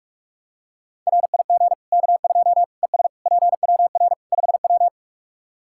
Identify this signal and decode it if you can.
Morse code, heard as a keyed tone, about 44 wpm, 700 Hz. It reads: RIG K2 ES PWR 5W
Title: Practice QSO - Exchange 2 - Info - 45wpm